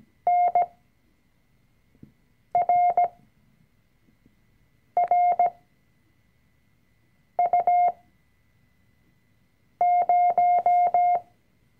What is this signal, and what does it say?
Morse code, heard as a keyed tone, about 17 wpm, spaced as for 4 wpm, 710 Hz. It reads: NRRU0
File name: CW-Online-Kurs / Anfänger / Lektion 004